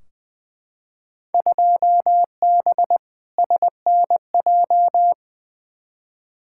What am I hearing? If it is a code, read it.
Morse code, heard as a keyed tone, about 20 wpm, 700 Hz. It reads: 2B SNJ
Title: Field Day Exchanges Encore - 20wpm